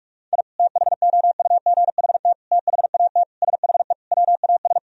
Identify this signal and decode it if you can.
Morse code, heard as a keyed tone, about 46 wpm, 700 Hz. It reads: I THOUGHT THAT SHE WAS
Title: Sentences from Top 1000 Words - Review Part 3 - 45wpm